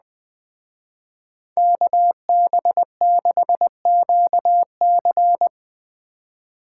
Morse code, heard as a keyed tone, about 20 wpm, 700 Hz. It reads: KB6QC
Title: Speed Racing - USA 2x2 Call Signs 17wpm